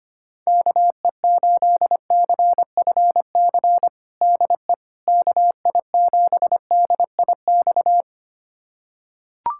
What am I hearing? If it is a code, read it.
Morse code, heard as a keyed tone, about 25 wpm, 700 Hz. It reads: KE8CFC DE KI7DIX
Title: USA Callsign DE Callsign - 6 Characters - 25wpm